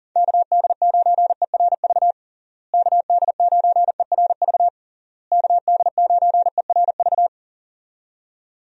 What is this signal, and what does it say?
Morse code, heard as a keyed tone, about 40 wpm, 700 Hz. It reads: KD9ERV KD9ERV KD9ERV
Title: USA Call Signs - 6 Characters - Repeated 3x - 40wpm